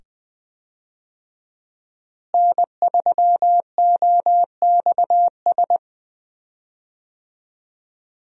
Morse code, heard as a keyed tone, about 20 wpm, 700 Hz. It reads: N3OXS
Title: USA Call Signs - 5 Characters - Repeated 2x - 20wpm